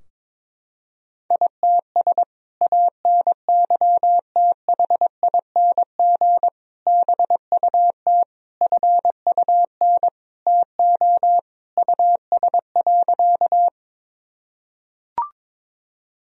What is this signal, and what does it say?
Morse code, heard as a keyed tone, about 22 wpm, 700 Hz. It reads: ITS ANYTHING BUT FUN TO US.
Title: Sentences from Top 1000 Words - Review Part 4 - 22wpm